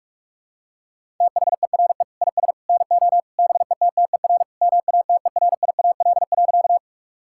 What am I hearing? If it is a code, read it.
Morse code, heard as a keyed tone, about 42 wpm, 700 Hz. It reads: THERE IS NO BETTER MATERIAL.
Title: Sentences from Top 1000 Words - Review Part 5 - 45wpm